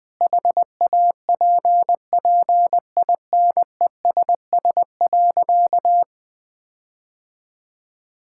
Morse code, heard as a keyed tone, about 21 wpm, 700 Hz. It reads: HAPPINESS.
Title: Sentences with 6-Word Phrases - Spoken Beginning - 20wpm